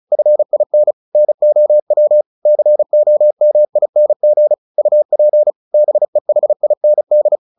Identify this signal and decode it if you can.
Morse code, heard as a keyed tone, about 34 wpm, 600 Hz. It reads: FIN NOW COMING UP BEHIND
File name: the old man and the sea 2 35wpm